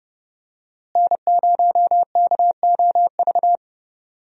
Morse code, heard as a keyed tone, about 30 wpm, 700 Hz. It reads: N0KOV
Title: USA Call Signs - 5 Characters - Repeated 3x - 30wpm